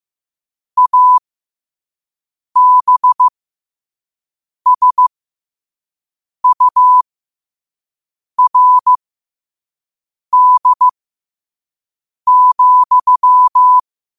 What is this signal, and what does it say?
Morse code, heard as a keyed tone, about 15 wpm, spaced as for 5 wpm, 1000 Hz. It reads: ABSURD,
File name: 5wpm